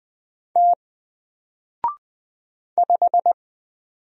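Morse code, heard as a keyed tone, about 20 wpm, 700 Hz. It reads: T 5
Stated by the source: Single letter and number Morse drill at 20 wpm, from N4LCD.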